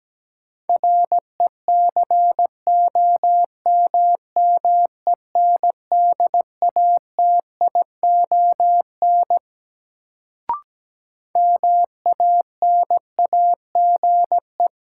Morse code, heard as a keyed tone, about 17 wpm, 700 Hz. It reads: RECOMMENDATION MANAGE
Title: Sentences from Top 1400 Words - New Word Drill - 17wpm